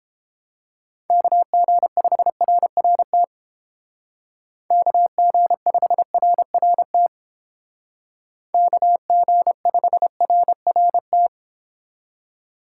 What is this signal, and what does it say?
Morse code, heard as a keyed tone, about 31 wpm, 700 Hz. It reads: KG5RRT KG5RRT KG5RRT